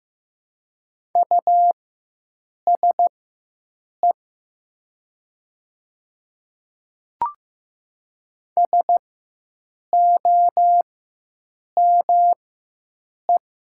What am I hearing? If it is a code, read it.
Morse code, heard as a keyed tone, about 15 wpm, spaced as for 7 wpm, 700 Hz. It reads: USE SOME
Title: 4x Character Spacing - Top 200 Words - 15wpm